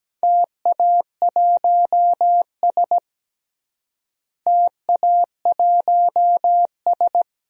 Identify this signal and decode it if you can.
Morse code, heard as a keyed tone, about 17 wpm, 700 Hz. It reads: TA1S TA1S
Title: DXCC-100 Call Signs - 4 Characters - Repeated 3x - 17wpm